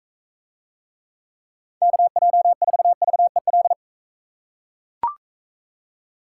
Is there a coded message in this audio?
Yes: Morse code from 1.8 to 3.7 s, heard as a keyed tone, about 38 wpm, 700 Hz. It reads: KJ4VEL